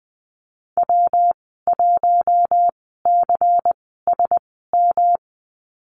Morse code, heard as a keyed tone, about 19 wpm, spaced as for 14 wpm, 700 Hz. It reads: W1CSM
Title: 167 Lesson 27g - Number 7 - Call Signs 20wpm